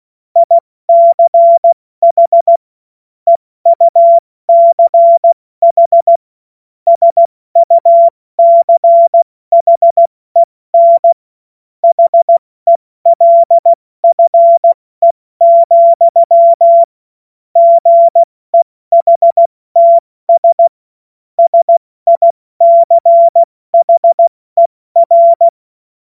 Morse code, heard as a keyed tone, about 16 wpm, 680 Hz. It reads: ICH EUCH SUCHEN HELFE, GEHTS SICHER
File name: Morsekurs / Witze / Witz 20. Januar 2026 70 F80